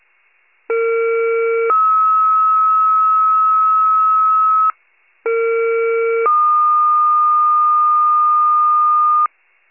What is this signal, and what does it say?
Two-tone page, heard as a one-second tone, then a three-second tone, about 460 then 1290 Hz, about 460 then 1220 Hz. It is heard from 0.7 to 9.3 s.
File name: wcalert